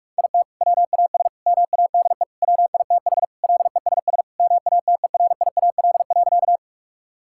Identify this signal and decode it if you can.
Morse code, heard as a keyed tone, about 41 wpm, 700 Hz. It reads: IT WAS MADE WITH LESS MATERIAL.
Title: Sentences from Top 1000 Words - Review Part 5 - 45wpm